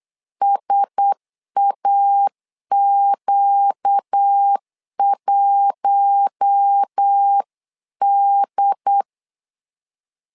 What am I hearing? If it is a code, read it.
Morse code, heard as a keyed tone, about 8 wpm, 800 Hz. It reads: SAQ1D
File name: LEZIONI / 4 Q1S 8wpm 2710